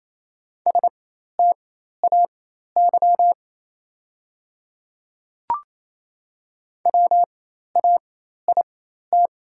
Morse code, heard as a keyed tone, about 28 wpm, spaced as for 13 wpm, 690 Hz. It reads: STAY WAIT
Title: 4x Character Spacing - Top 500 Words - 28wpm